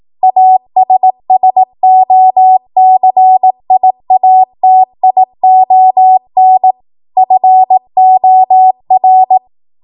Morse code, heard as a keyed tone, about 18 wpm, 750 Hz. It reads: ASSOCIATION FOR